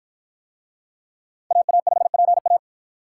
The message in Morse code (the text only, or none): AA5PA